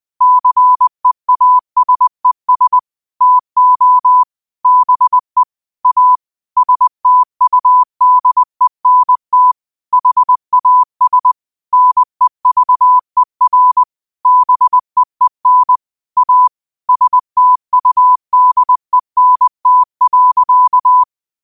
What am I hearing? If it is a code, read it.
Morse code, heard as a keyed tone, about 20 wpm, 1000 Hz. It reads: CEASES TO BE A STUDENT HAS NEVER BEEN A STUDENT.
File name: qotdfiles / 20wpm